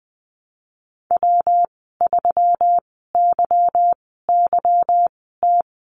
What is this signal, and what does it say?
Morse code, heard as a keyed tone, about 20 wpm, spaced as for 15 wpm, 700 Hz. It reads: W3YYT
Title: 186 Lesson 30g - Letter b - Call Signs 20wpm